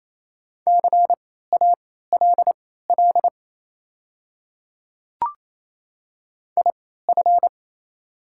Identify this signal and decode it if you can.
Morse code, heard as a keyed tone, about 27 wpm, spaced as for 16 wpm, 700 Hz. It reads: CALL IF
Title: 3x Character Spacing - Top 100 Words - 28wpm